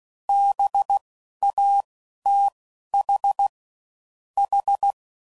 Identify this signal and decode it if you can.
Morse code, heard as a keyed tone, about 16 wpm, 800 Hz. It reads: B A T H H
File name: Suffixes / General / Houses